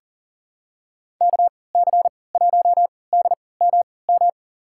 Morse code, heard as a keyed tone, about 40 wpm, spaced as for 23 wpm, 700 Hz. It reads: KC1DMM
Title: USA Call Signs - 6 Characters - 3x Character Spacing - 40wpm